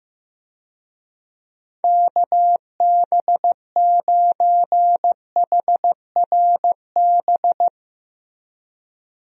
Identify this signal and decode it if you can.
Morse code, heard as a keyed tone, about 15 wpm, 700 Hz. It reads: KB9HRB
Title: USA Call Signs - 6 Characters - Repeated 2x - 15wpm